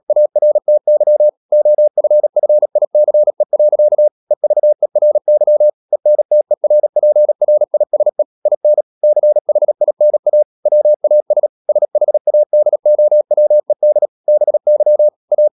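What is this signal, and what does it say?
Morse code, heard as a keyed tone, about 37 wpm, 600 Hz. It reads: ARTY OFFICE. EVERY ENTERPRISE IN CHINA WAS SHADOWED BY A
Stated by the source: Morse code practice at 35–39 wpm sending English text of Ken Follett's novel Never, from CW med Gnister.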